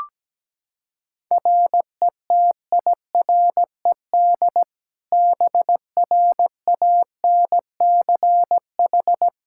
Morse code, heard as a keyed tone, about 17 wpm, 700 Hz. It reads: RETIRED BRANCH
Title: Practice QSO - Exchange 3 - More Info - 17wpm